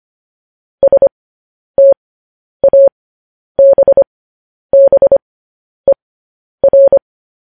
Morse code, heard as a keyed tone, about 25 wpm, spaced as for 10 wpm, 550 Hz. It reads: STABBER
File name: podcast / News headlines-1x-25x10